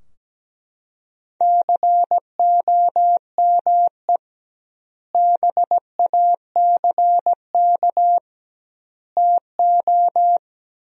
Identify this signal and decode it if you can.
Morse code, heard as a keyed tone, about 17 wpm, 700 Hz. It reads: COME BACK TO
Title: 2x Word-Spacing - Sentences from Top 500 Words - 17wpm